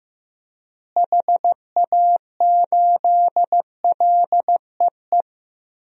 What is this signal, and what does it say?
Morse code, heard as a keyed tone, about 15 wpm, 700 Hz. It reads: HA8LEE
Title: DXCC-100 Call Signs - 6 Characters - Repeated 2x - 15wpm